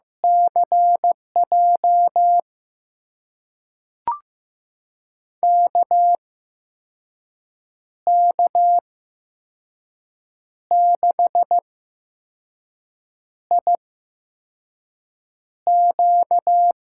Morse code, heard as a keyed tone, about 15 wpm, spaced as for 5 wpm, 700 Hz. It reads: CJ KK6IQ
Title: USA Call Signs - 6 Characters - 8x Character Spacing - 15wpm